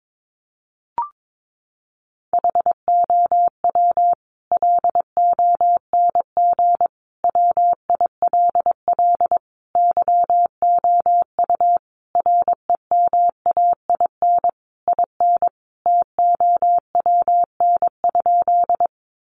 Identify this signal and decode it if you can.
Morse code, heard as a keyed tone, about 22 wpm, 700 Hz. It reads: HOW LONG WILL YOU REMAIN IN TOWN?
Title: Sentences from Top 1200 Words 22wpm